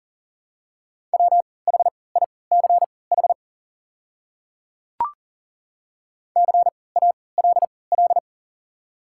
Morse code, heard as a keyed tone, about 40 wpm, spaced as for 23 wpm, 690 Hz. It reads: WHICH CALL
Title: 3x Character Spacing - Top 200 Words - 40wpm